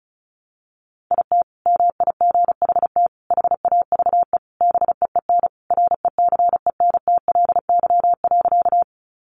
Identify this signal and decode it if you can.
Morse code, heard as a keyed tone, about 34 wpm, 700 Hz. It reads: IT MIGHT HAVE BEEN RECENTLY.
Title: Sentences from Top 1200 Words 35wpm